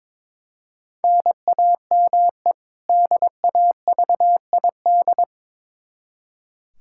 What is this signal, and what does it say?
Morse code, heard as a keyed tone, about 22 wpm, 700 Hz. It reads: NAME DAVID
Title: K1USN SST QSO Practice 22wpm